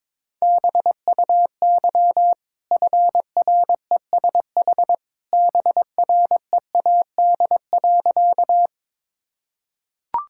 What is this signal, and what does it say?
Morse code, heard as a keyed tone, about 22 wpm, 690 Hz. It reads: BUY FRESH BREAD.
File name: practice / 22wpm / Easy Sentences with Hints - 22wpm